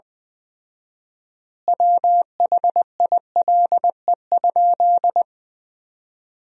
Morse code, heard as a keyed tone, about 20 wpm, 700 Hz. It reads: WHILE?
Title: 4x Word-Spacing - Sentences from Top 300 Words - 20wpm